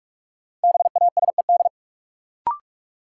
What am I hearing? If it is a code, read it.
Morse code, heard as a keyed tone, about 47 wpm, 700 Hz. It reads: BASED
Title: Sets of 2 Words - New Word Drill - 45wpm